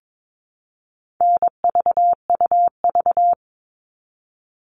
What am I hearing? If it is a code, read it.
Morse code, heard as a keyed tone, about 22 wpm, 700 Hz. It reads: NVUV
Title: All Cut Numbers - 4-Digit 22wpm